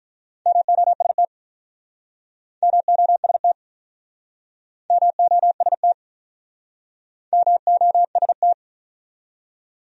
Morse code, heard as a keyed tone, about 48 wpm, 700 Hz. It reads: MOST MOST MOST MOST